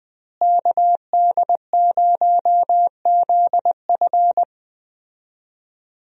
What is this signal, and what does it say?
Morse code, heard as a keyed tone, about 20 wpm, 700 Hz. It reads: KD0ZF
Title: USA Call Signs - 5 Characters - Repeated 3x - 20wpm